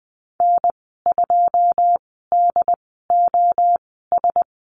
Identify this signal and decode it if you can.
Morse code, heard as a keyed tone, about 20 wpm, spaced as for 14 wpm, 700 Hz. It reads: N2DOS
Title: 088 Lesson 15g - Number 5 - Call Signs 20wpm